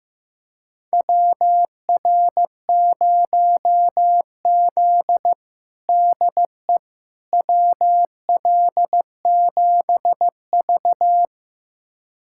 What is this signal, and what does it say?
Morse code, heard as a keyed tone, about 15 wpm, 700 Hz. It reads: WR0Z DE WL7V
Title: USA Callsign DE Callsign - 4 Characters - 15wpm